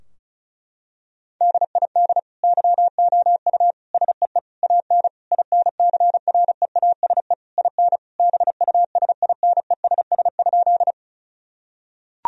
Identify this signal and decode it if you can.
Morse code, heard as a keyed tone, about 36 wpm, 700 Hz. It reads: DID YOU SEE AN INCREASE IN BUSINESS?
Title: Sentences from Top 1000 Words - Review Part 4 - 35wpm